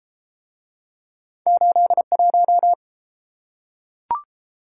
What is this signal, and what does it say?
Morse code, heard as a keyed tone, about 29 wpm, 700 Hz. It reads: 81